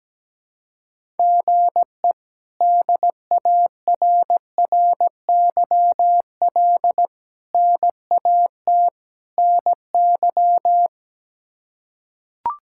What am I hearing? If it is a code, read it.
Morse code, heard as a keyed tone, about 17 wpm, 700 Hz. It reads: GE DARRYL NAT NY
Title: K1USN SST QSO Practice - 17wpm